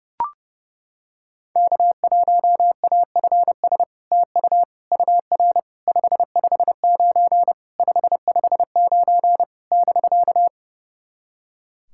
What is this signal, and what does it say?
Morse code, heard as a keyed tone, about 30 wpm, 700 Hz. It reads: K1AFS TU UR 559 559 <BK>
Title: SOTA QSO Practice 30wpm